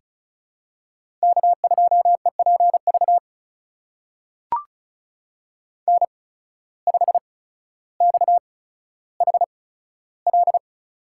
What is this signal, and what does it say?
Morse code, heard as a keyed tone, about 36 wpm, spaced as for 15 wpm, 700 Hz. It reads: K2EPV N5XHL
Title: USA Call Signs - 8x Character Spacing - 35wpm